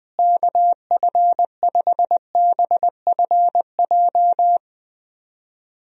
Morse code, heard as a keyed tone, about 20 wpm, 700 Hz. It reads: KF5BFJ
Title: USA Call Signs - 6 Characters - Repeated 3x - 20wpm